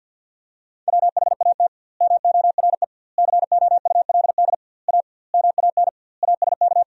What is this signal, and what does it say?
Morse code, heard as a keyed tone, about 50 wpm, 700 Hz. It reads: WHAT MORE COULD A MAN ASK
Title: 2x Word-Spacing - Sentences from Top 200 Words - 50wpm